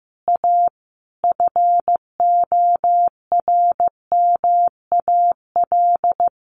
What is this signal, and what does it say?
Morse code, heard as a keyed tone, about 15 wpm, 700 Hz. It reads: A FORMAL
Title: Sentences from Top 1300 Words 15wpm